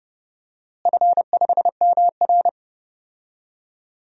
Morse code, heard as a keyed tone, about 30 wpm, 700 Hz. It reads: F5MR
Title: DXCC-100 Call Signs - 4 Characters - Repeated 3x - 30wpm